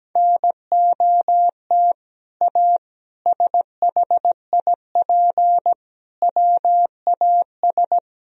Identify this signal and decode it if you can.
Morse code, heard as a keyed tone, about 17 wpm, 700 Hz. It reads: NOT A SHIP WAS
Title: Sentences from Top 1000 Words - Review Part 2 - 17wpm